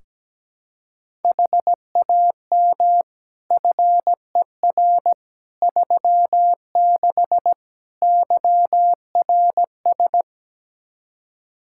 Morse code, heard as a keyed tone, about 17 wpm, 700 Hz. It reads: HAM FER 36 YRS
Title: Practice QSO - Exchange 3 - More Info - 17wpm